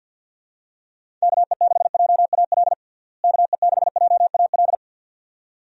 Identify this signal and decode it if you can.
Morse code, heard as a keyed tone, about 52 wpm, 700 Hz. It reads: KE6JAL KE6JAL
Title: USA Call Signs - 6 Characters - Repeated 2x - 50wpm